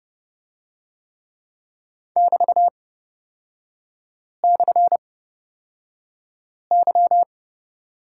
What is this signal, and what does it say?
Morse code, heard as a keyed tone, about 30 wpm, spaced as for 5 wpm, 700 Hz. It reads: <BT>/Y
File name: Lesson 36e - BT - Three Characters 30wpm